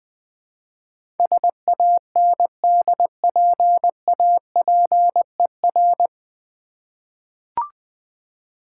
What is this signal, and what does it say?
Morse code, heard as a keyed tone, about 20 wpm, 700 Hz. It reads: SANDPAPER
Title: Difficult Words with Hints - 20wpm